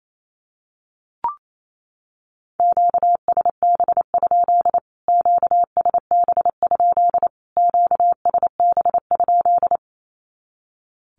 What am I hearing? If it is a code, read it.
Morse code, heard as a keyed tone, about 28 wpm, 700 Hz. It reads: QSB? QSB? QSB?
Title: Q-Codes Repeated 3x 28wpm